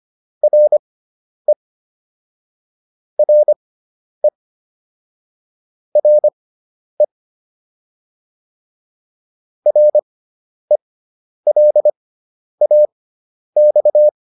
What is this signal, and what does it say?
Morse code, heard as a keyed tone, about 25 wpm, spaced as for 10 wpm, 600 Hz. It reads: RE RE RE RELAX